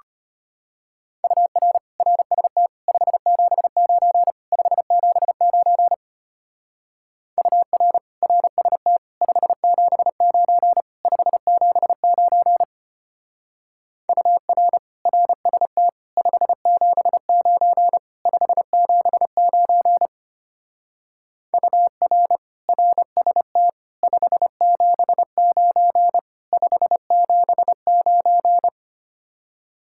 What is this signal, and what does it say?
Morse code, heard as a keyed tone, about 30 wpm, 700 Hz. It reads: UR RST 579 579 UR RST 579 579 UR RST 579 579 UR RST 579 579
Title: Speed-Racing Common QSO Phrases - 25wpm